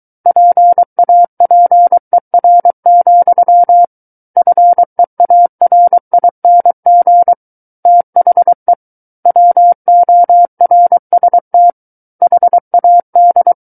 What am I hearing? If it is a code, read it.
Morse code, heard as a keyed tone, about 23 wpm, 700 Hz. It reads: PAPER, FEARING THE WORST HAD
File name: Book One / 14-Chapter 14 - 23 WPM